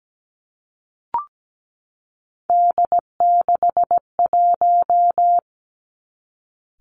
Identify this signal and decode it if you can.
Morse code, heard as a keyed tone, about 17 wpm, 700 Hz. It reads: D61
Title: Three Challenging Letter-Number Combinations 17wpm